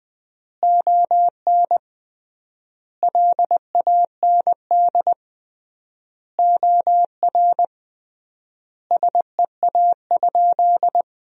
Morse code, heard as a keyed tone, about 20 wpm, 700 Hz. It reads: ON LAND OR SEA?
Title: 3x Word-Spacing - Sentences from Top 500 Words - 20wpm